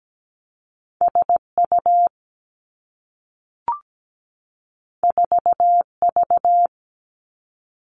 Morse code, heard as a keyed tone, about 17 wpm, 700 Hz. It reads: SU 4V
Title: Two Challenging Letter-Number Combinations 17wpm